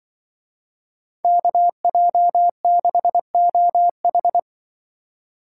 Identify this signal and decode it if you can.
Morse code, heard as a keyed tone, about 24 wpm, 700 Hz. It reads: KJ6OH
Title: Speed Racing - USA 2x2 Call Signs 20wpm